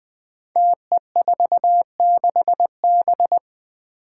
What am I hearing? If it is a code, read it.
Morse code, heard as a keyed tone, about 20 wpm, 700 Hz. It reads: TE46B
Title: Five Challenging Letter-Number Combinations 20wpm